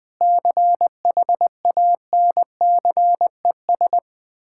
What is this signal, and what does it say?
Morse code, heard as a keyed tone, about 20 wpm, 700 Hz. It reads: CHANCES
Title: Sentences with 8-Word Phrases - 20wpm